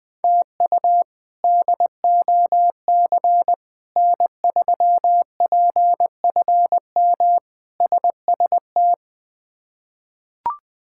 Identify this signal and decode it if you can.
Morse code, heard as a keyed tone, about 21 wpm, 700 Hz. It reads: TU DOC N3PFM SST
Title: K1USN SST QSO Practice - 20wpm